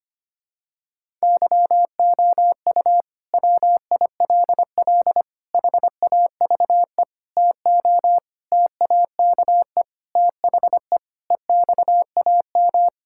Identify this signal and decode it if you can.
Morse code, heard as a keyed tone, about 25 wpm, 700 Hz. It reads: YOU WILL HAVE TO TAKE THE EXAM
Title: Sentences from Top 1000 Words - Review Part 5 - 25wpm